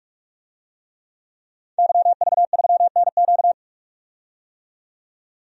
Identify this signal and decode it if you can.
Morse code, heard as a keyed tone, about 46 wpm, 700 Hz. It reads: YV3NQ
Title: DXCC-100 Call Signs - 5 Characters - 45wpm